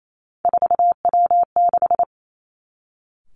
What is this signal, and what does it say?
Morse code, heard as a keyed tone, about 28 wpm, 700 Hz. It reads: 4W6
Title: Three Challenging Letter-Number Combinations 28wpm